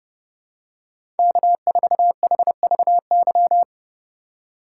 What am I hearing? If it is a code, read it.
Morse code, heard as a keyed tone, about 32 wpm, 700 Hz. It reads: K4HVY
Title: USA Call Signs - 5 Characters - 2x Character Spacing - 30wpm